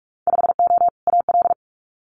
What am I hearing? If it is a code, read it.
Morse code, heard as a keyed tone, about 46 wpm, 700 Hz. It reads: 5O AL
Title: Winter Field Day Exchanges 45wpm